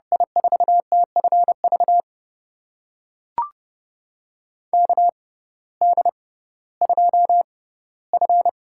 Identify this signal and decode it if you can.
Morse code, heard as a keyed tone, about 29 wpm, spaced as for 16 wpm, 700 Hz. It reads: I4TFV KD2F
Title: USA Call Signs - 6 Characters - 6x Character Spacing - 30wpm